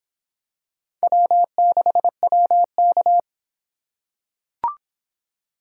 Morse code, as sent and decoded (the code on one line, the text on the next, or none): .-- -.... .-- -.-
W6WK